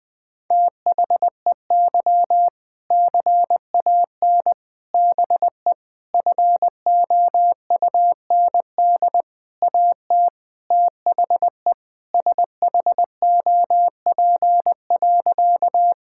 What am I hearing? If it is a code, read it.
Morse code, heard as a keyed tone, about 20 wpm, 700 Hz. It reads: THEY CAN BE FOUND AT THE SHOP.
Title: Sentences from Top 1600 Words 20wpm